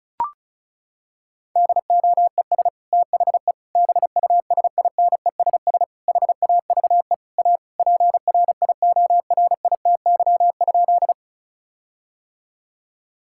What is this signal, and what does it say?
Morse code, heard as a keyed tone, about 34 wpm, 700 Hz. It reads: DOES THE BUSINESS HAVE A PRIORITY?
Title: Sentences from Top 1600 Words 35wpm